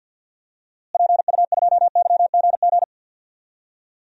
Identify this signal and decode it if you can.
Morse code, heard as a keyed tone, about 52 wpm, 700 Hz. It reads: PU2YGG
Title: DXCC-100 Call Signs - 6 Characters - Repeated 3x - 50wpm